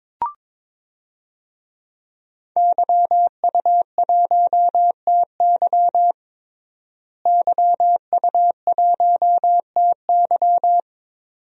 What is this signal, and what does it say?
Morse code, heard as a keyed tone, about 23 wpm, 700 Hz. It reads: YU1TY YU1TY
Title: DXCC-100 Call Signs - 5 Characters - Repeated 2x - 22wpm